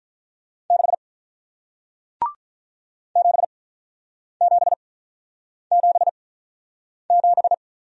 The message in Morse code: 6 7777

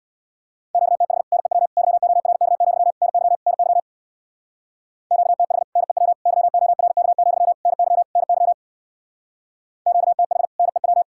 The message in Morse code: QTH NEW YORK, NY NY QTH NEW YORK, NY NY QTH NEW

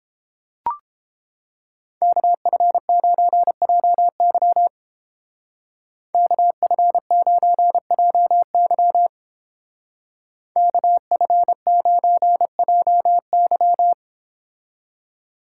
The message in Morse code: KF9JY KF9JY KF9JY

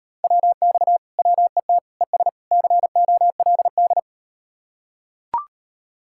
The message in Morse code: WX WET ES COLD

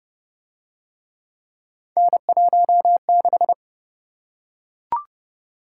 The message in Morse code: N16

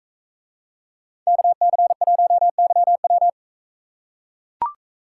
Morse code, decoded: KC1YW

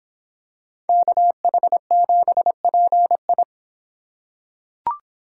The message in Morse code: KH7PI

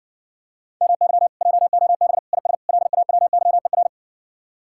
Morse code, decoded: MY JOB IS LAWYER

